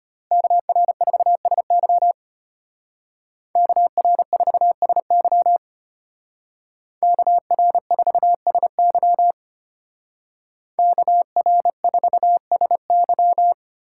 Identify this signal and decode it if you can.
Morse code, heard as a keyed tone, about 33 wpm, 700 Hz. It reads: KR4SY KR4SY KR4SY KR4SY